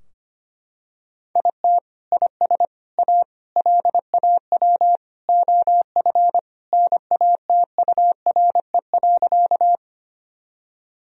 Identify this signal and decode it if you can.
Morse code, heard as a keyed tone, about 25 wpm, 700 Hz. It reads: IT IS A LAW OF NATURE.
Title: Sentences from Top 1000 Words - Review Part 3 - 25wpm